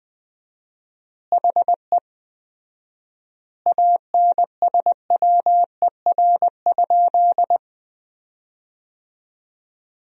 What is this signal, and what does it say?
Morse code, heard as a keyed tone, about 19 wpm, 700 Hz. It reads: HE ANSWER?
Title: 4x Word-Spacing - Sentences from Top 300 Words - 20wpm